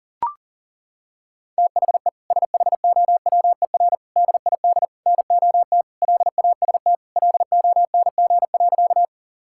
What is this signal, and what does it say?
Morse code, heard as a keyed tone, about 40 wpm, 700 Hz. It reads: THE SHOWER DID NOT LAST LONG.
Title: Sentences from Top 1600 Words 40wpm